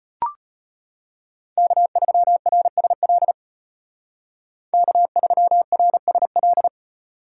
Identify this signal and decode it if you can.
Morse code, heard as a keyed tone, about 32 wpm, 700 Hz. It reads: K3RSL K3RSL